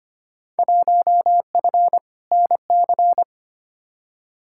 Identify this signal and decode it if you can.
Morse code, heard as a keyed tone, about 25 wpm, 700 Hz. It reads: 1F NC
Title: Field Day Exchanges - 25wpm